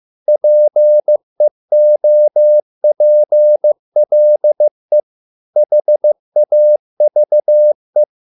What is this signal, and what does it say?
Morse code, heard as a keyed tone, about 15 wpm, 600 Hz. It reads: PEOPLE HAVE